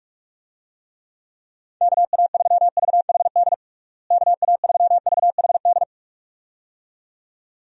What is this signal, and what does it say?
Morse code, heard as a keyed tone, about 46 wpm, 700 Hz. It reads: KA3VHD KA3VHD
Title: USA Call Signs - 6 Characters - Repeated 2x - 45wpm